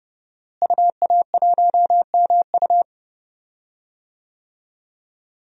Morse code, heard as a keyed tone, about 32 wpm, 700 Hz. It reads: UA1MU
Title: DXCC-100 Call Signs - 5 Characters - Repeated 3x - 30wpm